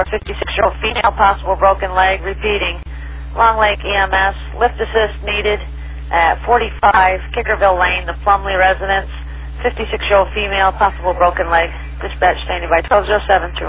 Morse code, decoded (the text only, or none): none